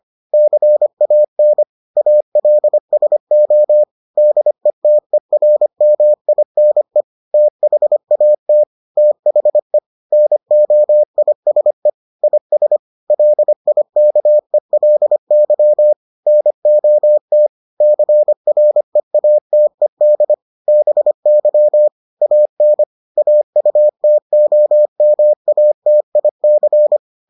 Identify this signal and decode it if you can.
Morse code, heard as a keyed tone, about 25 wpm, 600 Hz. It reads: CAN ALSO DETERMINE THAT THE NOISE IS LIKELY NOT CREATED BY AN AUTOMATIC